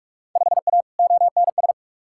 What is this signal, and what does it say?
Morse code, heard as a keyed tone, about 45 wpm, 700 Hz. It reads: 5A ONS
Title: Field Day Exchanges Encore - 45wpm